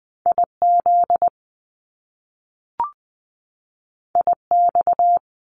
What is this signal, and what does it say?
Morse code, heard as a keyed tone, about 20 wpm, 700 Hz. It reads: I Z I X
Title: Top Two Letter Combinations in Words 20wpm